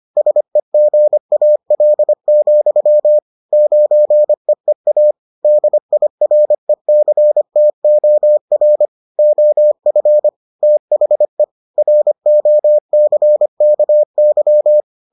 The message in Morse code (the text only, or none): SEGAL, 9EEA DIRECTOR OF THE ROCKY